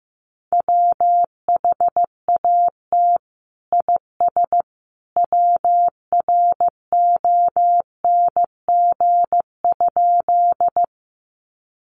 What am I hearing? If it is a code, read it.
Morse code, heard as a keyed tone, about 15 wpm, 700 Hz. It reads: WHAT IS WRONG?
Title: Sentences from Top 900 Words 15wpm